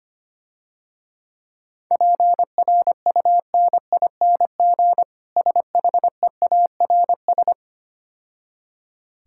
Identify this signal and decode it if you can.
Morse code, heard as a keyed tone, about 26 wpm, 690 Hz. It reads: PRUNING SHEARS
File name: Easy Words - Speed-Racing with Hints - 25wpm